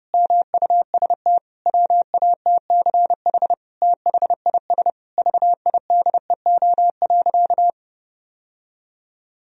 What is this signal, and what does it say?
Morse code, heard as a keyed tone, about 32 wpm, 700 Hz. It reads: MUST WATCH THIS VIDEO.
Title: Sentences from Top 1000 Words - Review Part 4 - 30wpm